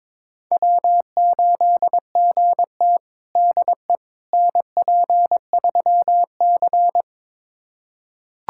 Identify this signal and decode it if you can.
Morse code, heard as a keyed tone, about 22 wpm, 700 Hz. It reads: W8GT DE NP3C
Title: USA Callsign DE Callsign - 4 Characters - 22wpm